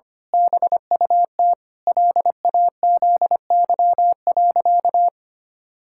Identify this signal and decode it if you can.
Morse code, heard as a keyed tone, about 25 wpm, 700 Hz. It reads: BUT LAZY.
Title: Pangram Challenge v2 - 25wpm